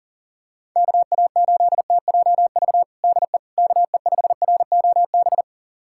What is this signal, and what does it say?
Morse code, heard as a keyed tone, about 40 wpm, 700 Hz. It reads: KA8TJV DE KE5ROB
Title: USA Callsign DE Callsign 40wpm